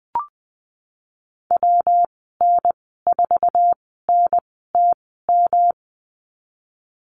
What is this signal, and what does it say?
Morse code, heard as a keyed tone, about 20 wpm, spaced as for 15 wpm, 700 Hz. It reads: WN4NTM
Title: 122 Lesson 20g - Number 3 - Call Signs 20wpm